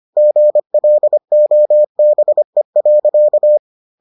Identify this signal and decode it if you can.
Morse code, heard as a keyed tone, about 25 wpm, 600 Hz. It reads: GLOBE.